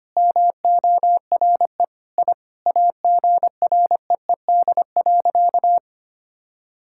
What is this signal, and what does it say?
Morse code, heard as a keyed tone, about 25 wpm, 700 Hz. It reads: MORE I AGREED.
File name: practice / 25wpm / Sentences with 9-Word Phrases - 25wpm